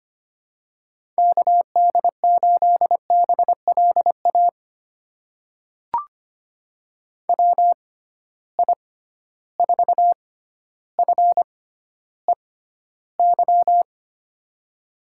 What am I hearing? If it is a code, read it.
Morse code, heard as a keyed tone, about 25 wpm, spaced as for 13 wpm, 700 Hz. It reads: KD8BLA WI4FEY
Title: USA Call Signs - 6 Characters - 6x Character Spacing - 25wpm